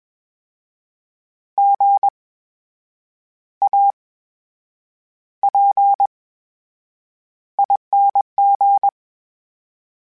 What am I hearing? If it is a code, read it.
Morse code, heard as a keyed tone, about 21 wpm, spaced as for 8 wpm, 800 Hz. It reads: GAPING
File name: Special INGs-mp3s / Special INGs-mp3s / VerySlowToFaster Ing @20